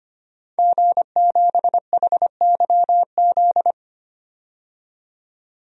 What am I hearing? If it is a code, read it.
Morse code, heard as a keyed tone, about 25 wpm, 700 Hz. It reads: G7HYZ
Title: DXCC-100 Call Signs - 5 Characters - 25wpm